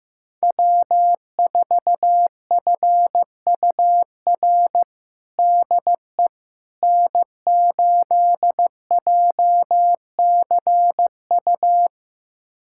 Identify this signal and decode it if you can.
Morse code, heard as a keyed tone, about 15 wpm, 700 Hz. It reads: W4FUR DE N8JCU
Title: USA Callsign DE Callsign - 5 Characters - 15wpm